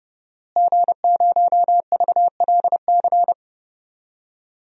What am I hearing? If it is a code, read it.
Morse code, heard as a keyed tone, about 32 wpm, 700 Hz. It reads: G0VLC
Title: DXCC-100 Call Signs - 5 Characters - Repeated 3x - 30wpm